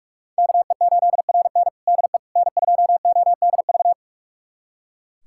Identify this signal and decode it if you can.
Morse code, heard as a keyed tone, about 44 wpm, 700 Hz. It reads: KE8RN DE N2ODV
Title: USA Callsign DE Callsign - 5 Characters - 45wpm